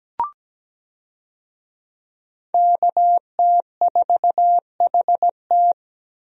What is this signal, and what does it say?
Morse code, heard as a keyed tone, about 17 wpm, 700 Hz. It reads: KT4HT
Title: USA Call Signs - 5 Characters - Repeated 2x - 17wpm